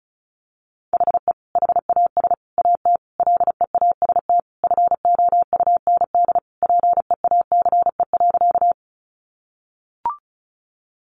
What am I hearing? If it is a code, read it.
Morse code, heard as a keyed tone, about 34 wpm, 700 Hz. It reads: HE HAS AT LEAST FOUND PEACE.
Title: Sentences from Top 1500 Words 35wpm